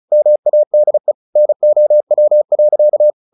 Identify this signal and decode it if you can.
Morse code, heard as a keyed tone, about 34 wpm, 600 Hz. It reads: MADE NOW.
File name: the old man and the sea / the old man and the sea 1 35wpm